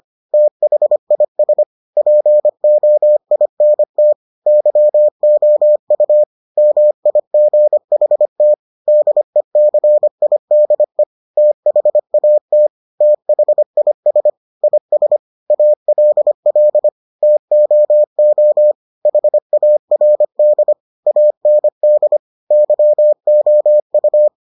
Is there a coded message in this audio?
Yes: Morse code from 0.3 to 24.3 s, heard as a keyed tone, about 25 wpm, 600 Hz. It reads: THIS POINT YOU MIGHT DECIDE THAT THIS IS ALL TOO HARD AND YOU